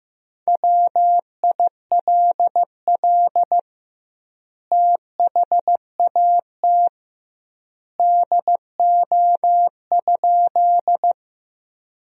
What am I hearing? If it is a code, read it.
Morse code, heard as a keyed tone, about 15 wpm, 700 Hz. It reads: WILL THAT DO?
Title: 2x Word-Spacing - Sentences from Top 100 Words - 15wpm